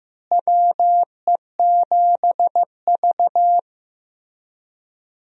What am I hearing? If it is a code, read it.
Morse code, heard as a keyed tone, about 15 wpm, 700 Hz. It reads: WE7V
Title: USA Call Signs - 4 Characters - Repeated 2x - 15wpm